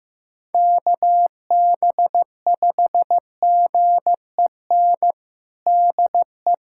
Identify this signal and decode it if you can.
Morse code, heard as a keyed tone, about 15 wpm, 700 Hz. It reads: KB5GEN DE
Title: USA Callsign DE Callsign - 6 Characters - 15wpm